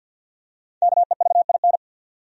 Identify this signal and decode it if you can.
Morse code, heard as a keyed tone, about 53 wpm, 700 Hz. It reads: KEVIN
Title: USA Common Names - Max 5 Letters - 50wpm